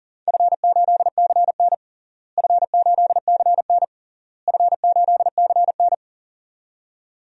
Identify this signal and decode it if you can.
Morse code, heard as a keyed tone, about 40 wpm, 700 Hz. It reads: F8CN F8CN F8CN
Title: DXCC-100 Call Signs - 4 Characters - Repeated 3x - 40wpm